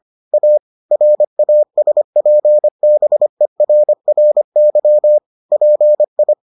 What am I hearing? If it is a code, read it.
Morse code, heard as a keyed tone, about 25 wpm, 600 Hz. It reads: A RASPBERRY PI